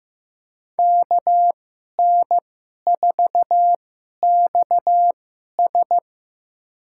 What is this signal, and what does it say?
Morse code, heard as a keyed tone, about 15 wpm, spaced as for 11 wpm, 700 Hz. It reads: KN4XS
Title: USA Call Signs - 5 Characters - 2x Character Spacing - 15wpm